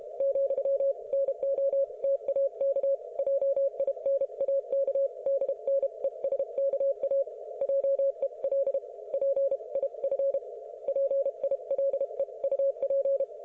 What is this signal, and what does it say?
Morse code, heard as a keyed tone, about 32 wpm, 550 Hz. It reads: ,NOTAK JINAK DNESKA JEL PIF PILEUP